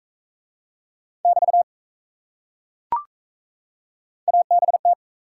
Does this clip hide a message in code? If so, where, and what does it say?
Yes: Morse code from 1.3 to 4.9 s, heard as a keyed tone, about 35 wpm, 700 Hz. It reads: <BT> ABT